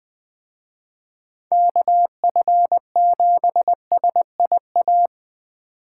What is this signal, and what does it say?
Morse code, heard as a keyed tone, about 20 wpm, 700 Hz. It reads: KF7SIA